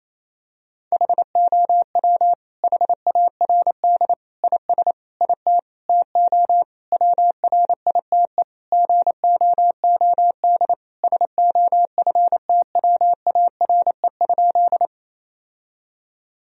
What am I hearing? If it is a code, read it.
Morse code, heard as a keyed tone, about 28 wpm, 700 Hz. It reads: HOW HARD IS IT TO WRITE GOOD SOFTWARE?
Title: Sentences from Top 1000 Words - Review Part 3 - 28wpm